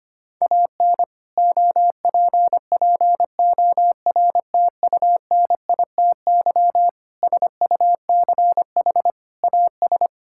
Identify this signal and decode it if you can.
Morse code, heard as a keyed tone, about 25 wpm, 700 Hz. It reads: AN OPPORTUNITY SUCH AS
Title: Sentences from Top 1600 Words 25wpm